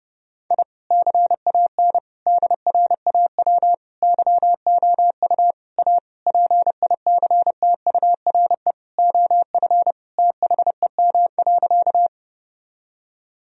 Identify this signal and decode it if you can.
Morse code, heard as a keyed tone, about 30 wpm, 700 Hz. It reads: I CAN DRAW YOU A PICTURE OF THEM.
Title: Sentences from Top 1000 Words - Review Part 1 - 30wpm